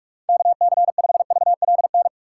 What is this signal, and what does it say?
Morse code, heard as a keyed tone, about 46 wpm, 700 Hz. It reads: KC5VLN
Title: USA Call Signs - 6 Characters - 6x Character Spacing - 45wpm